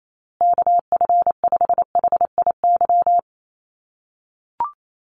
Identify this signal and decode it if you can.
Morse code, heard as a keyed tone, about 28 wpm, 700 Hz. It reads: KF5HIY
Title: USA Call Signs Repeated 3x 28wpm